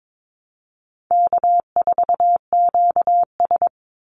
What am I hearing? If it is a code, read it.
Morse code, heard as a keyed tone, about 22 wpm, 700 Hz. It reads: K4QS
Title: NA SOTA Activator Callsigns 22wpm